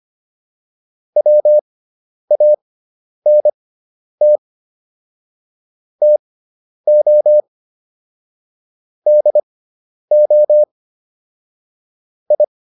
Morse code, heard as a keyed tone, about 25 wpm, spaced as for 10 wpm, 600 Hz. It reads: WANT TO DO I
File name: WD401 10